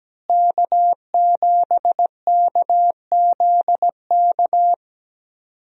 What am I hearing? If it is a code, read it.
Morse code, heard as a keyed tone, about 17 wpm, 700 Hz. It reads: K7KZK
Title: USA Call Signs - 5 Characters - Repeated 2x - 17wpm